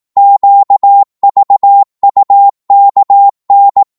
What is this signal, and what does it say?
Morse code, heard as a keyed tone, about 18 wpm, 800 Hz. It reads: QVUKN
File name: Morse / ltr-18wpm